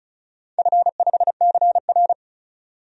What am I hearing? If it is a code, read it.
Morse code, heard as a keyed tone, about 36 wpm, 700 Hz. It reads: F5CR
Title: DXCC-100 Call Signs - 4 Characters - 35wpm